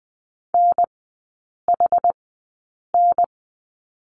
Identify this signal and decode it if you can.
Morse code, heard as a keyed tone, about 20 wpm, spaced as for 8 wpm, 700 Hz. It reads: NHN
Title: 060 Lesson 11e - Letter h - Three Characters 20wpm